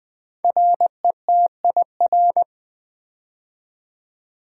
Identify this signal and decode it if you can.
Morse code, heard as a keyed tone, about 20 wpm, 700 Hz. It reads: RETIR
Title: Build A Word - 5 Letters At A Time - 10-Letter Words - 20wpm